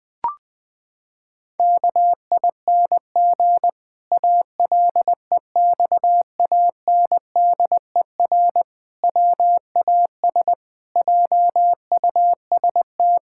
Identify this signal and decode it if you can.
Morse code, heard as a keyed tone, about 20 wpm, 700 Hz. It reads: KING ALEXANDER WAS JUST
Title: Pangram Challenge v2 - 20wpm